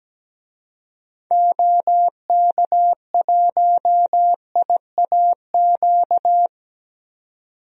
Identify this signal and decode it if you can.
Morse code, heard as a keyed tone, about 17 wpm, 700 Hz. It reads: OK1IAQ
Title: DXCC-100 Call Signs - 6 Characters - 17wpm